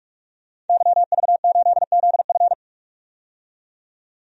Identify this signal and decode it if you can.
Morse code, heard as a keyed tone, about 45 wpm, 700 Hz. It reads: YV8ZF
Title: DXCC-100 Call Signs - 5 Characters - 45wpm